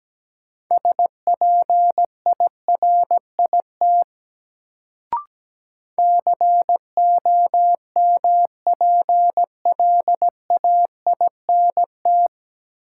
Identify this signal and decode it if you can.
Morse code, heard as a keyed tone, about 17 wpm, 700 Hz. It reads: SPIRIT COMPLAINT
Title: Sentences from Top 1400 Words - New Word Drill - 17wpm